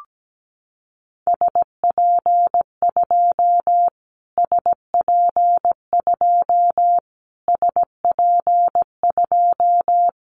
Morse code, heard as a keyed tone, about 17 wpm, 700 Hz. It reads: SP2 SP2 SP2
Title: Callsign Prefixes for Top 25 DXCC - Repeated 3x - 17wpm